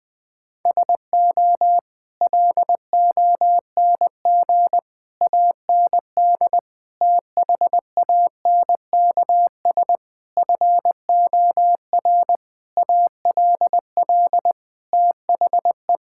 Morse code, heard as a keyed tone, about 20 wpm, 700 Hz. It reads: SO LONG AND THANKS FOR ALL THE
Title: Sentences with 8-Word Phrases - 20wpm